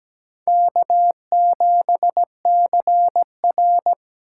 Morse code, heard as a keyed tone, about 17 wpm, 700 Hz. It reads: K7CR
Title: USA Call Signs - 4 Characters - 4x Character Spacing - 17wpm